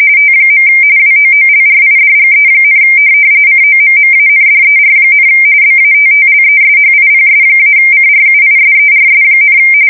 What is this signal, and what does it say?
RTTY, about 45.45 baud. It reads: T. WOOLRIDGE, COMMANDER, BATTLESHIPS-CRUISERS,
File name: fsk170 / USS-Missouri